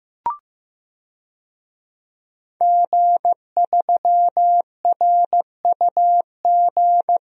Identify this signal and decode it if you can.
Morse code, heard as a keyed tone, about 15 wpm, 700 Hz. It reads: G3RUG
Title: DXCC-100 Call Signs - 5 Characters - Repeated 2x - 15wpm